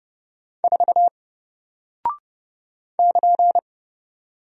Morse code, heard as a keyed tone, about 31 wpm, 700 Hz. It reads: 4 <KN>
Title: Single Character (Letter:Number:Punctuation:Prosign) 30wpm